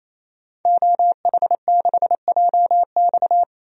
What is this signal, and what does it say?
Morse code, heard as a keyed tone, about 28 wpm, 700 Hz. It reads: OH6JX
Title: DXCC-100 Call Signs - 5 Characters - 28wpm